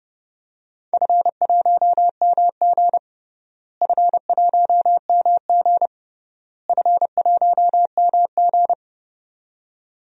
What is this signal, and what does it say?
Morse code, heard as a keyed tone, about 30 wpm, 700 Hz. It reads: F1MG F1MG F1MG
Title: DXCC-100 Call Signs - 4 Characters - Repeated 3x - 30wpm